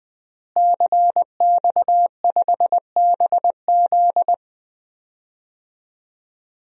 Morse code, heard as a keyed tone, about 20 wpm, 700 Hz. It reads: CX5BZ
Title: DXCC-100 Call Signs - 5 Characters - Repeated 2x - 20wpm